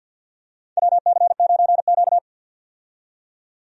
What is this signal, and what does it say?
Morse code, heard as a keyed tone, about 47 wpm, 700 Hz. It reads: WC9Q
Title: USA Call Signs - 4 Characters - 50wpm